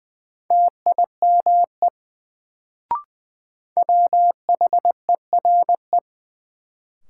Morse code, heard as a keyed tone, about 20 wpm, 700 Hz. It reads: TIME WHERE
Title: Sets of 2 Words - New Word Drill - 20wpm